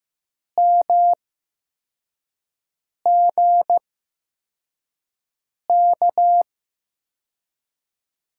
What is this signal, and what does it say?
Morse code, heard as a keyed tone, about 15 wpm, spaced as for 4 wpm, 700 Hz. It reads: MGK
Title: USA Call Signs - 6 Characters - 8x Character Spacing - 15wpm